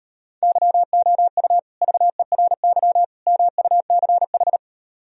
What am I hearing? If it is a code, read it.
Morse code, heard as a keyed tone, about 38 wpm, 700 Hz. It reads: YOU VERY MUCH